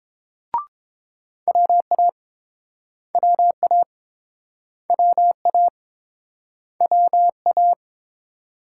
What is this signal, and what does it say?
Morse code, heard as a keyed tone, about 30 wpm, 700 Hz. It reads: WA WA WA WA